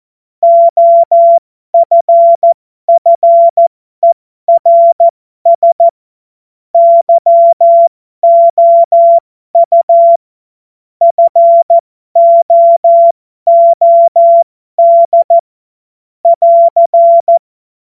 Morse code, heard as a keyed tone, about 14 wpm, spaced as for 12 wpm, 680 Hz. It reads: OFFERS YOU FOOD <AR>
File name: Morsekurs / Witze